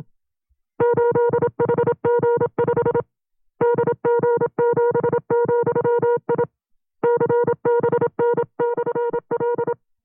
Morse code, heard as a keyed tone, about 27 wpm, 480 Hz. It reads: 8HG5 DG7,I CBN/L
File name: FreeFiles / CW-practice / 25WPM-side2